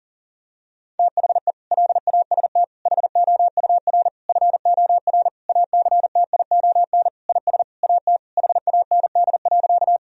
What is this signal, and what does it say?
Morse code, heard as a keyed tone, about 37 wpm, 700 Hz. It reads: THE LAST HOUR FOR ACTION IS AT HAND.
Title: Sentences from Top 1000 Words - Review Part 4 - 40wpm